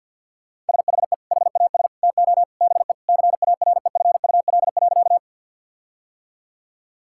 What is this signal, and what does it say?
Morse code, heard as a keyed tone, about 50 wpm, 700 Hz. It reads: SHE HAS TO BE CAREFUL.
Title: Sentences from Top 1600 Words 50wpm